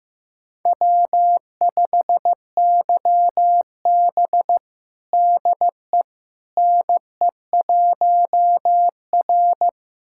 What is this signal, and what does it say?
Morse code, heard as a keyed tone, about 15 wpm, 700 Hz. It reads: W5YB DE NE1R
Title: USA Callsign DE Callsign - 4 Characters - 15wpm